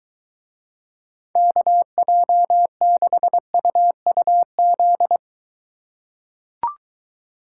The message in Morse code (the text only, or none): KJ6UUZ